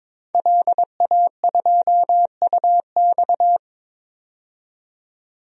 Morse code, heard as a keyed tone, about 22 wpm, 700 Hz. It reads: LA2UX
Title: DXCC-100 Call Signs - 5 Characters - 22wpm